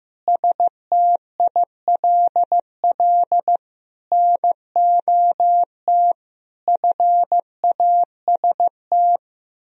Morse code, heard as a keyed tone, about 15 wpm, 700 Hz. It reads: STILL NOT FAST
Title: Sentences from Top 1000 Words - Review Part 3 - 15wpm